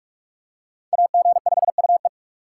Morse code, heard as a keyed tone, about 44 wpm, 700 Hz. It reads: AG5UE